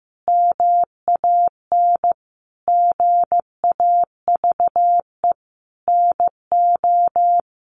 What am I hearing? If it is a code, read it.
Morse code, heard as a keyed tone, about 15 wpm, 700 Hz. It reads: MAN GAVE NO
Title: Sentences from Top 500 Words 15wpm